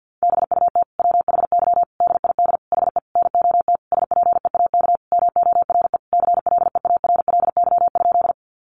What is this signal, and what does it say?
Morse code, heard as a keyed tone, about 50 wpm, 700 Hz. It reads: BUT WHY DID HE NOT SPEAK MORE CLEARLY?
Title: Sentences from Top 1300 Words 50wpm